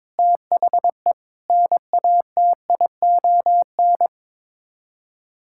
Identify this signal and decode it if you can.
Morse code, heard as a keyed tone, about 22 wpm, 700 Hz. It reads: THE NATION
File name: practice / 22wpm / Build A Phrase - 5-Word Phrases - 22wpm